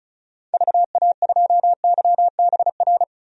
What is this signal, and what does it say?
Morse code, heard as a keyed tone, about 34 wpm, 700 Hz. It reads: VA2YBR
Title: DXCC-100 Call Signs - 6 Characters - 35wpm